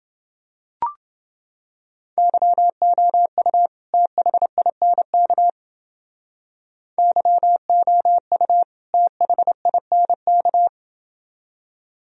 Morse code, heard as a keyed tone, about 28 wpm, 700 Hz. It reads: YOU THINK YOU THINK